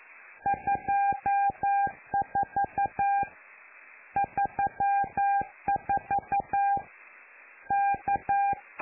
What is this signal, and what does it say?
Morse code, heard as a keyed tone, about 11 wpm, 800 Hz. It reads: 24 34 K
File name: lu4aao / Globo Amsat